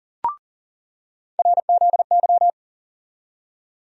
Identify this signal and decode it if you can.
Morse code, heard as a keyed tone, about 40 wpm, 700 Hz. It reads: RZY
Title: USA Call Sign Postfixes - 3 Letters 40wpm